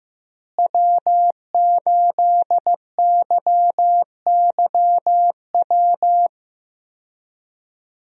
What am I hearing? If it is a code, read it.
Morse code, heard as a keyed tone, about 15 wpm, 700 Hz. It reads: W8YYW
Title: USA Call Signs - 5 Characters - Repeated 3x - 15wpm